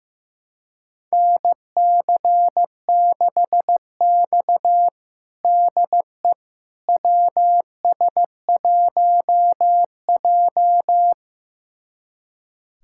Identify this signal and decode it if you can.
Morse code, heard as a keyed tone, about 16 wpm, 700 Hz. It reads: NC6X DE WS1J
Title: USA Callsign DE Callsign - 4 Characters - 15wpm